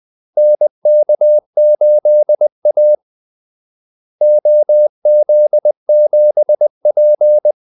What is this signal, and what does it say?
Morse code, heard as a keyed tone, about 20 wpm, 600 Hz. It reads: NK8A OZ7P
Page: Kaldesignaler 20 wpm